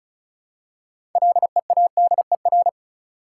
Morse code, heard as a keyed tone, about 35 wpm, 700 Hz. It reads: LEADER
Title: Six Letter Words - 35wpm